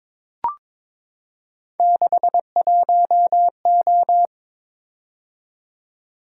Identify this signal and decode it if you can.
Morse code, heard as a keyed tone, about 22 wpm, 700 Hz. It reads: 61O
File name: practice / 22wpm / Three Letter-Number Combinations - 22wpm